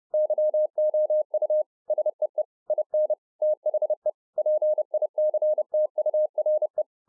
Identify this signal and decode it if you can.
Morse code, heard as a keyed tone, about 30 wpm, 610 Hz. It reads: YOU SEE IN THE PICTURE